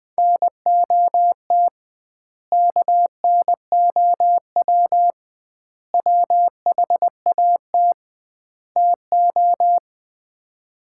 Morse code, heard as a keyed tone, about 20 wpm, 700 Hz. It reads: NOT KNOW WHAT TO
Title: 2x Word-Spacing - Sets of 4 Words - 20wpm